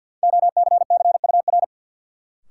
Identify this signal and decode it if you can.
Morse code, heard as a keyed tone, about 49 wpm, 700 Hz. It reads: OCCUR
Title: Sentences from Top 900 Words - New Word Drill - 50wpm